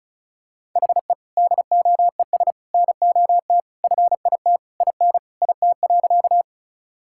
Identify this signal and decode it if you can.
Morse code, heard as a keyed tone, about 36 wpm, 700 Hz. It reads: HE DOES NOT FIT IN IT.
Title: Sentences from Top 1000 Words - Review Part 5 - 35wpm